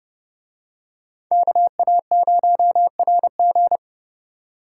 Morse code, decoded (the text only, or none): KA0RG